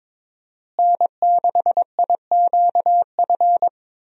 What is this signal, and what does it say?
Morse code, heard as a keyed tone, about 22 wpm, 700 Hz. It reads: N6IQF
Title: USA Call Signs - 5 Characters - Repeated 3x - 22wpm